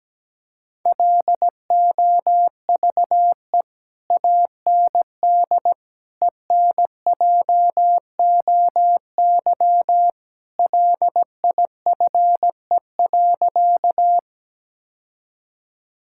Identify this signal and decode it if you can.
Morse code, heard as a keyed tone, about 17 wpm, 700 Hz. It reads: LOVE AND ENJOY LIFE.
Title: Sentences from Top 1000 Words - Review Part 5 - 17wpm